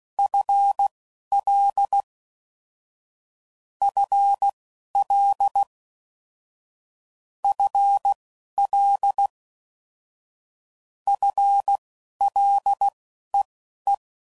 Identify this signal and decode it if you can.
Morse code, heard as a keyed tone, about 16 wpm, spaced as for 11 wpm, 800 Hz. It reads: FL FL FL FLEE
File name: Prefixes / FLs & THs / FL Prefixes 3